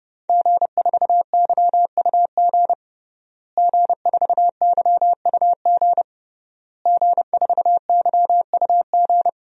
Morse code, heard as a keyed tone, about 30 wpm, 700 Hz. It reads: G4YUG G4YUG G4YUG
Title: DXCC-100 Call Signs - 5 Characters - Repeated 3x - 30wpm